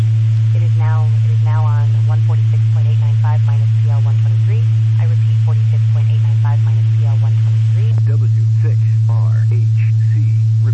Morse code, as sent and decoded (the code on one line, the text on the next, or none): none